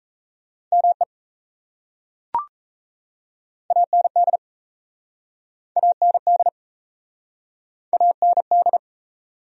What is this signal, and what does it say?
Morse code, heard as a keyed tone, about 38 wpm, 700 Hz. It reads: ME AND AND AND